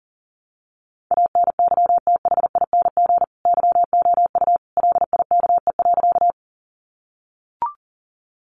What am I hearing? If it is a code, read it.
Morse code, heard as a keyed tone, about 40 wpm, 700 Hz. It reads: ANYTHING YOU LIKE.
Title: Sentences from Top 1000 Words 40wpm